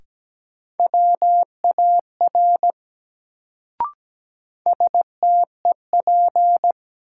Morse code, heard as a keyed tone, about 17 wpm, 700 Hz. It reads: WAR STEP
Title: Sentences from Top 400 Words - New Word Drill - 17wpm